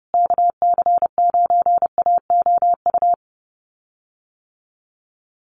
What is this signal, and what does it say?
Morse code, heard as a keyed tone, about 30 wpm, 700 Hz. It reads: KC9AOU
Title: USA Call Signs Repeated 3x 30wpm